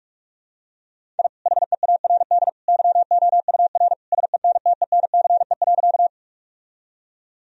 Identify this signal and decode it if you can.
Morse code, heard as a keyed tone, about 46 wpm, 700 Hz. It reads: I HEARD YOUR SENTENCE.
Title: Sentences from Top 1000 Words - Review Part 1 - 45wpm